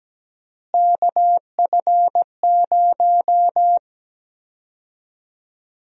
Morse code, heard as a keyed tone, about 17 wpm, 700 Hz. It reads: KF0
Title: USA Call Sign Prefixes - Repeated 2x - 17wpm